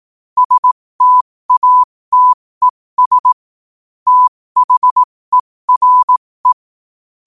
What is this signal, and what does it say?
Morse code, heard as a keyed tone, about 18 wpm, spaced as for 15 wpm, 1000 Hz. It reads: STATES THERE
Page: qotd-at-15-wpm